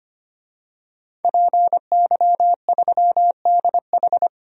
Morse code, heard as a keyed tone, about 25 wpm, 700 Hz. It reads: PY3DH
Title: DXCC-100 Call Signs - 5 Characters - 25wpm